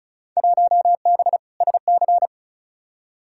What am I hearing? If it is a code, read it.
Morse code, heard as a keyed tone, about 36 wpm, 700 Hz. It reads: 1B SC
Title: Field Day Exchanges Encore - 35wpm